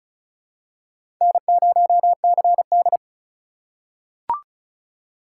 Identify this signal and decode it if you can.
Morse code, heard as a keyed tone, about 36 wpm, 700 Hz. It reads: N0CD
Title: USA Call Signs - 4 Characters - 3x Character Spacing - 35wpm